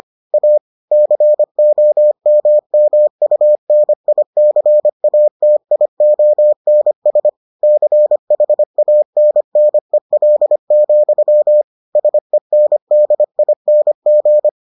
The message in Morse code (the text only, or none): A COMMUNICATIONS CHANNEL, SENDING